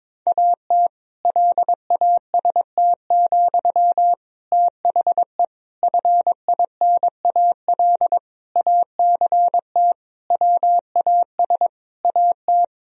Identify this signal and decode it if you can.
Morse code, heard as a keyed tone, about 21 wpm, 700 Hz. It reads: AT LAST, THE FINAL ACT WAS AT
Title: Sentences from Top 1000 Words - Review Part 2 - 22wpm